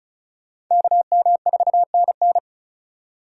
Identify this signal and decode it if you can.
Morse code, heard as a keyed tone, about 44 wpm, 700 Hz. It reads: KM4NN